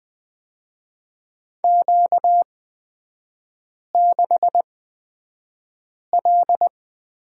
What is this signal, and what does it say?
Morse code, heard as a keyed tone, about 20 wpm, spaced as for 5 wpm, 690 Hz. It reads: Q6L
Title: 236 Lesson 38e - Letter q - Three Characters 20wpm